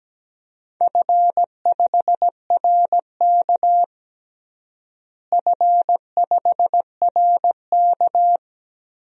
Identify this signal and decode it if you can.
Morse code, heard as a keyed tone, about 17 wpm, 700 Hz. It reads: F5RK F5RK
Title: DXCC-100 Call Signs - 4 Characters - Repeated 3x - 17wpm